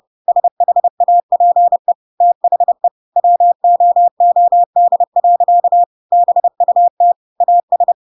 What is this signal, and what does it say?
Morse code, heard as a keyed tone, about 30 wpm, 700 Hz. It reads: SHAPE THE WOOD. BUT AS
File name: Pinocchio / Pinocchio ch1 30 30 ik1hge